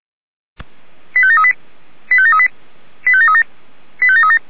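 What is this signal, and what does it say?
Five-tone selective call (ZVEI), heard as quick runs of five tones, tones of about 70 ms. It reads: 85738 85738 85737 85737
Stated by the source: ZVEI five-tone call received by SDR.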